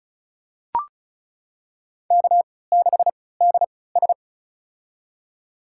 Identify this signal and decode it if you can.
Morse code, heard as a keyed tone, about 36 wpm, spaced as for 20 wpm, 700 Hz. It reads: K6DS
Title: USA Call Signs - 4 Characters - 3x Character Spacing - 35wpm